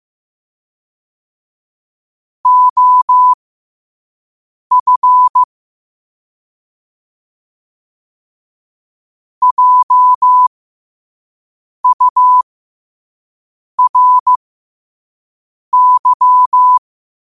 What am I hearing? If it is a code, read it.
Morse code, heard as a keyed tone, about 15 wpm, spaced as for 5 wpm, 1000 Hz. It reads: OF JURY
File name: qotdfiles / 5wpm